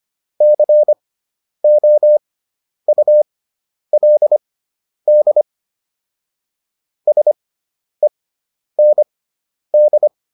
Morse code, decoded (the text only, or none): COULD SEND